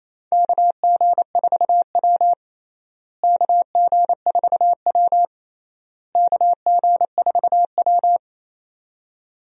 Morse code, heard as a keyed tone, about 28 wpm, 700 Hz. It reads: KG4W KG4W KG4W
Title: USA Call Signs - 4 Characters - Repeated 3x - 28wpm